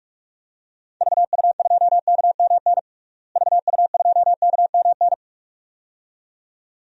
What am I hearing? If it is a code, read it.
Morse code, heard as a keyed tone, about 41 wpm, 690 Hz. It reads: VU2KMN VU2KMN
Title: DXCC-100 Call Signs - 6 Characters - Repeated 2x - 45wpm